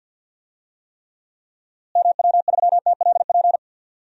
Morse code, heard as a keyed tone, about 46 wpm, 700 Hz. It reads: MW3TLP
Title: DXCC-100 Call Signs - 6 Characters - Repeated 2x - 50wpm